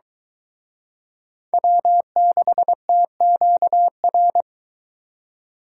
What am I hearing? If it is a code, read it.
Morse code, heard as a keyed tone, about 23 wpm, 700 Hz. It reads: W6TQR